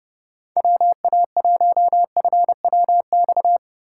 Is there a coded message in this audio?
Yes: Morse code from 0.6 to 3.6 s, heard as a keyed tone, about 28 wpm, 700 Hz. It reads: WA1FWX